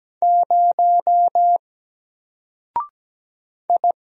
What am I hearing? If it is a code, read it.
Morse code, heard as a keyed tone, about 17 wpm, 700 Hz. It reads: 0 I
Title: Single Character (Letter:Number:Punctuation:Prosign) 17wpm